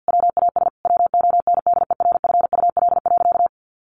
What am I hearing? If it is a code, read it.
Morse code, heard as a keyed tone, about 50 wpm, 700 Hz. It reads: WAS WONDERFUL.
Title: Sentences from Top 1100 Words 50wpm